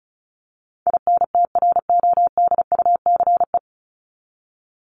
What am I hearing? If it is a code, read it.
Morse code, heard as a keyed tone, about 34 wpm, 700 Hz. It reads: INTRODUCE
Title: Top 1501-1750 Words 35wpm